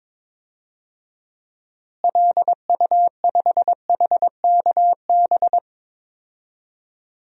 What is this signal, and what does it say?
Morse code, heard as a keyed tone, about 21 wpm, 700 Hz. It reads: LU5HKB
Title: DXCC-100 Call Signs - 6 Characters - Repeated 3x - 22wpm